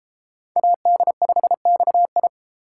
Morse code, heard as a keyed tone, about 30 wpm, 700 Hz. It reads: AD5XI